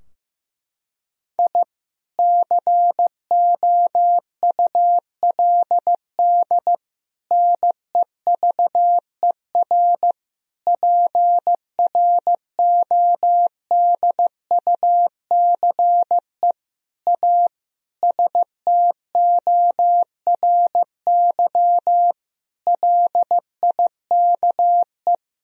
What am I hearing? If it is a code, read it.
Morse code, heard as a keyed tone, about 15 wpm, 700 Hz. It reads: I COULD NEVER PRODUCE A STORY LIKE
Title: Sentences from Top 1000 Words - Review Part 3 - 15wpm